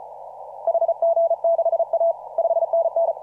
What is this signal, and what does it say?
Morse code, heard as a keyed tone, about 34 wpm, 690 Hz. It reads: HG6A 5NN
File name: CW / RHOB CW 2017 HA6P 0001